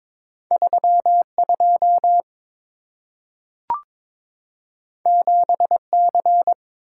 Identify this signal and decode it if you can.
Morse code, heard as a keyed tone, about 22 wpm, 700 Hz. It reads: 32 7C
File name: practice / 22wpm / Two Letter-Number Combinations - 22wpm